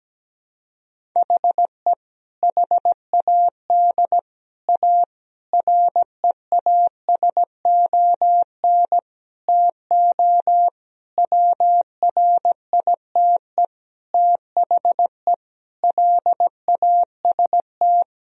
Morse code, heard as a keyed tone, about 17 wpm, 700 Hz. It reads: HE HAD A REASON TO WRITE THE LAST
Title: Sentences from Top 1000 Words - Review Part 1 - 17wpm